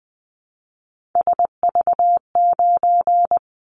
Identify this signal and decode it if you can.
Morse code, heard as a keyed tone, about 20 wpm, 700 Hz. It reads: SV9
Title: Callsign Prefixes for Top 100 DXCC - 20wpm